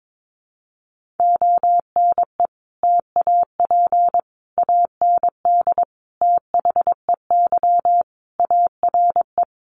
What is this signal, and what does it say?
Morse code, heard as a keyed tone, about 22 wpm, 700 Hz. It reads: ONE TAP AND THEY ARE
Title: Sentences from Top 1400 Words 22wpm